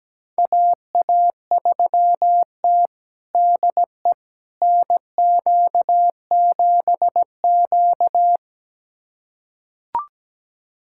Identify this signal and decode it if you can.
Morse code, heard as a keyed tone, about 17 wpm, 700 Hz. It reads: AA3T DE NQ7Q
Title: USA Callsign DE Callsign - 4 Characters - 17wpm